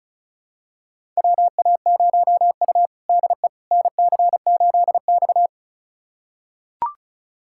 Morse code, heard as a keyed tone, about 34 wpm, 700 Hz. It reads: WA0U DE NC8X
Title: USA Callsign DE Callsign - 4 Characters - 35wpm